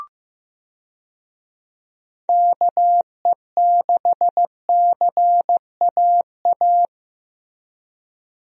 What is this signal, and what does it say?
Morse code, heard as a keyed tone, about 15 wpm, 700 Hz. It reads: KE6CAA
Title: USA Call Signs - 6 Characters - Repeated 3x - 15wpm